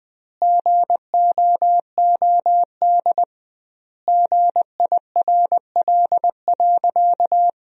Morse code, heard as a keyed tone, about 20 wpm, 700 Hz. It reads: GOOD GIRL.
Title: 2x Word-Spacing - Sentences from Top 400 Words - 20wpm